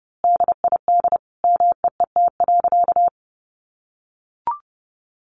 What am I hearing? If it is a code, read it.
Morse code, heard as a keyed tone, about 30 wpm, 700 Hz. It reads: DID MEET.
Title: Sentences from Top 1100 Words 30wpm